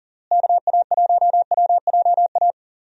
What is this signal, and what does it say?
Morse code, heard as a keyed tone, about 40 wpm, 700 Hz. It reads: KA1WJA
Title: USA Call Signs - 6 Characters - Repeated 3x - 40wpm